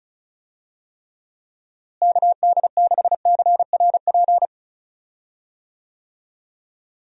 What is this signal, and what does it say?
Morse code, heard as a keyed tone, about 36 wpm, 700 Hz. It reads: KD6CRP
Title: USA Call Signs - 6 Characters - Repeated 3x - 35wpm